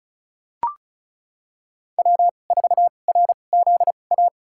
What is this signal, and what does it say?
Morse code, heard as a keyed tone, about 34 wpm, spaced as for 25 wpm, 700 Hz. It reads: W4RZA
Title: USA Call Signs - 2x Character Spacing - 35wpm